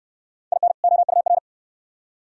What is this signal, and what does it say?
Morse code, heard as a keyed tone, about 69 wpm, 700 Hz. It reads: IN OUR